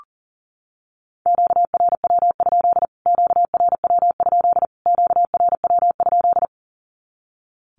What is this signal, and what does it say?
Morse code, heard as a keyed tone, about 40 wpm, 700 Hz. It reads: QRW? QRW? QRW?
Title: Q-Codes Repeated 3x 40wpm